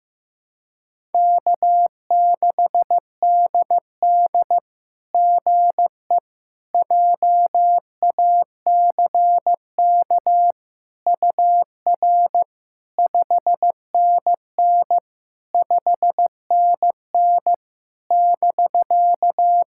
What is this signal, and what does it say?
Morse code, heard as a keyed tone, about 16 wpm, 700 Hz. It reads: K6DD GE JACK UR 5NN 5NN <BK>
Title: SOTA QSO Practice 15wpm